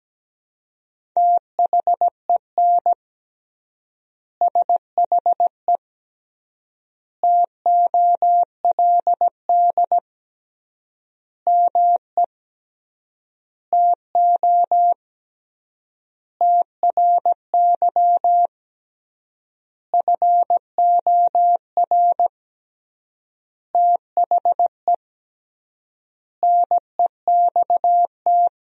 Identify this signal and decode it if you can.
Morse code, heard as a keyed tone, about 17 wpm, 700 Hz. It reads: THEN SHE TOLD ME TO TRY FOR THE NEXT
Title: 3x Word-Spacing - Sentences from Top 400 Words - 17wpm